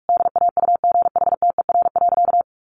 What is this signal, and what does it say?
Morse code, heard as a keyed tone, about 44 wpm, 700 Hz. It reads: DAUGHTER.
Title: Sentences from Top 1500 Words 45wpm